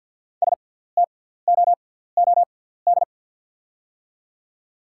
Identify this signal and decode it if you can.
Morse code, heard as a keyed tone, about 53 wpm, spaced as for 17 wpm, 690 Hz. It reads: STOOD
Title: 6x Character Spacing - Top 500 Words - 50wpm